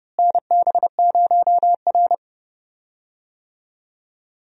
Morse code, heard as a keyed tone, about 32 wpm, 700 Hz. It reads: NB0R
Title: USA Call Signs - 4 Characters - Repeated 2x - 30wpm